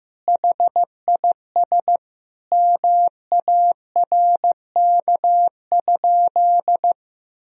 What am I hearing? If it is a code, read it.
Morse code, heard as a keyed tone, about 15 wpm, 700 Hz. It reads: HIS MARK?
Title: Sentences from Top 300 Words and Abbreviations 15wpm